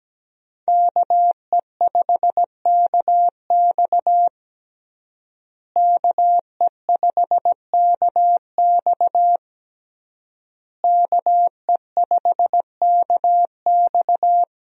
Morse code, heard as a keyed tone, about 17 wpm, 700 Hz. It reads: KE5KX KE5KX KE5KX
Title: USA Call Signs - 5 Characters - Repeated 3x - 17wpm